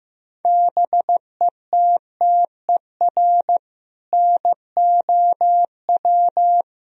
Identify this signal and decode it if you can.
Morse code, heard as a keyed tone, about 16 wpm, 700 Hz. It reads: BETTER NOW
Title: Sentences from Top 1000 Words - Review Part 3 - 15wpm